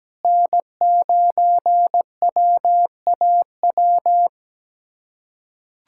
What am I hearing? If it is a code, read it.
Morse code, heard as a keyed tone, about 17 wpm, 700 Hz. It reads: N9WAW
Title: USA Callsign DE Callsign 17wpm